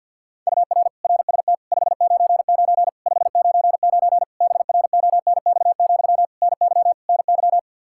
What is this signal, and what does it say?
Morse code, heard as a keyed tone, about 52 wpm, 700 Hz. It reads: UR RST 599 599 BRONX, NY NY
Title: Practice QSO - Exchange 1 - Beginning - 50wpm